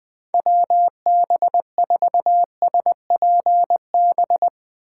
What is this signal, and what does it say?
Morse code, heard as a keyed tone, about 20 wpm, 700 Hz. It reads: WB4SPB
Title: K1USN SST QSO Practice - 20wpm